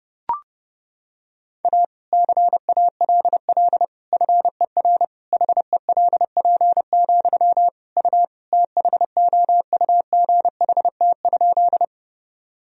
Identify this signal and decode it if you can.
Morse code, heard as a keyed tone, about 30 wpm, 700 Hz. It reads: A CALL FER HELP, U THOUGHT?
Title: Sentences from Top 300 Words and Abbreviations 30wpm